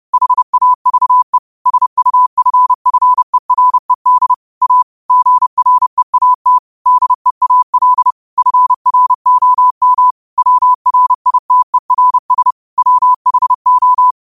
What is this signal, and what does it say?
Morse code, heard as a keyed tone, about 30 wpm, 1000 Hz. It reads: HAVE SUFFERED A GREAT DEAL FROM WRITERS WHO
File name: qotdfiles / 30wpm